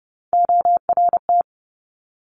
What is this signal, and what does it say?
Morse code, heard as a keyed tone, about 30 wpm, 700 Hz. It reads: ORT
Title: Top Three Letter Combinations in Words 30wpm